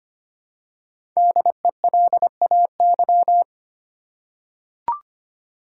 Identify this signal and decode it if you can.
Morse code, heard as a keyed tone, about 25 wpm, 700 Hz. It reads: DELAY
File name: 5 Letter Words - First 3 Letters Given - 25wpm